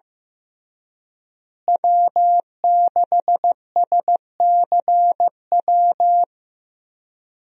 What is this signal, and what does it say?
Morse code, heard as a keyed tone, about 15 wpm, 700 Hz. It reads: W6SCW
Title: USA Call Signs - 5 Characters - Repeated 2x - 15wpm